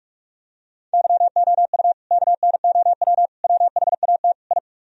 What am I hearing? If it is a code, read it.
Morse code, heard as a keyed tone, about 40 wpm, 700 Hz. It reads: YOU KNOW WHAT I